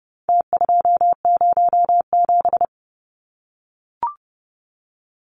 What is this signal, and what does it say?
Morse code, heard as a keyed tone, about 28 wpm, 700 Hz. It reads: T207
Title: Numbers - Contest Style - 4-Digit 30wpm